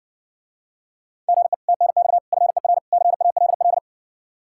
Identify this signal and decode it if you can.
Morse code, heard as a keyed tone, about 60 wpm, 700 Hz. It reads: GE TNX FER CALL